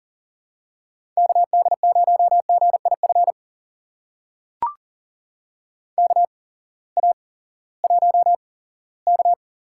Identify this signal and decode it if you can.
Morse code, heard as a keyed tone, about 40 wpm, spaced as for 20 wpm, 700 Hz. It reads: KD0GIF KA1K
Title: USA Call Signs - 6 Characters - 8x Character Spacing - 40wpm